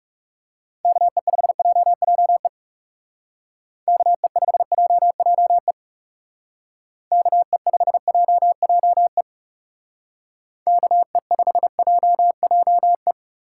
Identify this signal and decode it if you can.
Morse code, heard as a keyed tone, about 37 wpm, 700 Hz. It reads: KE5JJE KE5JJE KE5JJE KE5JJE
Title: Speed Racing - USA 2x3 Call Signs 30wpm